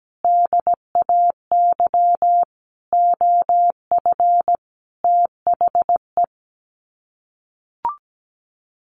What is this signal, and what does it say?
Morse code, heard as a keyed tone, about 17 wpm, 700 Hz. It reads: DAY OF THE
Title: Sets of 5 Words Encore2 17wpm